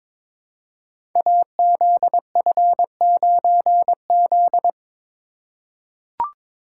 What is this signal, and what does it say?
Morse code, heard as a keyed tone, about 22 wpm, 700 Hz. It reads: AZF9Z
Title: Code Groups - 5 Letter:Numbers - 22wpm